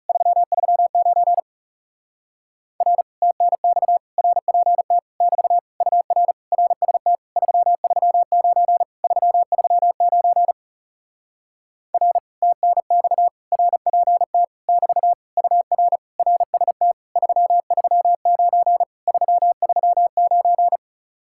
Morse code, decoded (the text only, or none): 339 R TNX RPT <BT> UR RST 339 339 R TNX RPT <BT> UR RST 339 339